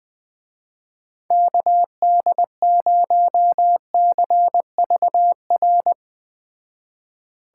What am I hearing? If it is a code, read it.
Morse code, heard as a keyed tone, about 20 wpm, 700 Hz. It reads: KD0CVR
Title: Speed Racing - USA 2x3 Call Signs 17wpm